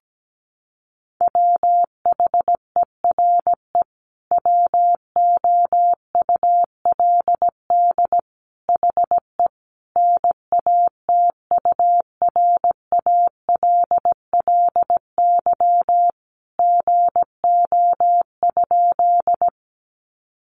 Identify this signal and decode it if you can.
Morse code, heard as a keyed tone, about 17 wpm, 700 Hz. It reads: WHERE WOULD HE NATURALLY GO?
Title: Sentences from Top 1300 Words 17wpm